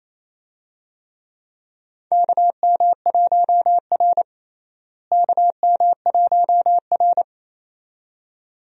Morse code, heard as a keyed tone, about 28 wpm, 700 Hz. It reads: KM1R KM1R
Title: USA Call Signs - 4 Characters - Repeated 2x - 28wpm